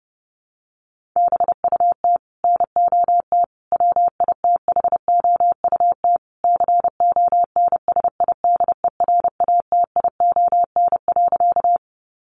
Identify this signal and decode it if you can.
Morse code, heard as a keyed tone, about 30 wpm, 700 Hz. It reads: BUT NOT WITHOUT CONSIDERATION.
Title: Sentences from Top 1500 Words 30wpm